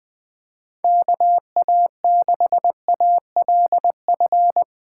Morse code, heard as a keyed tone, about 20 wpm, 700 Hz. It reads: KA6ALF